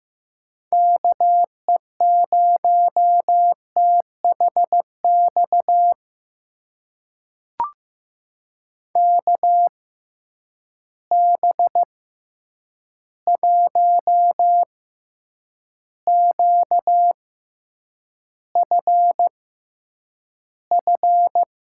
Morse code, heard as a keyed tone, about 15 wpm, spaced as for 8 wpm, 700 Hz. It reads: KE0THX KB1QFF
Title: USA Call Signs - 6 Characters - 6x Character Spacing - 15wpm